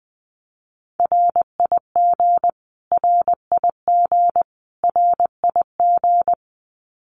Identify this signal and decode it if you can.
Morse code, heard as a keyed tone, about 20 wpm, 700 Hz. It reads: RIG RIG RIG
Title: Abbreviations Repeated 3x - 20wpm